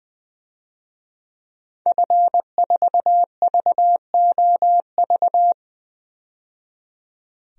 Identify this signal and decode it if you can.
Morse code, heard as a keyed tone, about 21 wpm, 700 Hz. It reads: F4VOV
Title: DXCC-100 Call Signs - 5 Characters - 20wpm